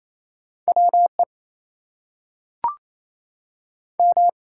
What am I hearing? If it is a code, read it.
Morse code, heard as a keyed tone, about 28 wpm, 700 Hz. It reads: WE M
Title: Two Letter Words - 28wpm